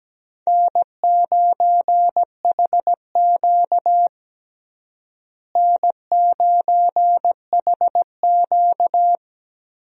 Morse code, heard as a keyed tone, about 17 wpm, 700 Hz. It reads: N9HQ N9HQ
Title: USA Call Signs - 4 Characters - Repeated 2x - 17wpm